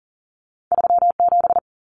Morse code, heard as a keyed tone, about 41 wpm, 700 Hz. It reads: 37
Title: Numbers - 2-Digit 40wpm